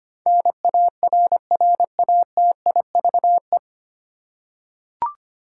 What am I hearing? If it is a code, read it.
Morse code, heard as a keyed tone, about 25 wpm, 700 Hz. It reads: NARRATIVE
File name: Nine Letter Words - 25wpm